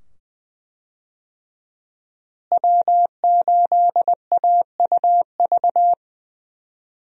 Morse code, heard as a keyed tone, about 20 wpm, 700 Hz. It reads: W8AUV
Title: USA Call Signs - 5 Characters - Repeated 2x - 20wpm